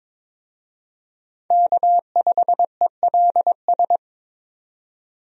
Morse code, heard as a keyed tone, about 21 wpm, 700 Hz. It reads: K5ELS
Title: USA Call Signs - 5 Characters - 22wpm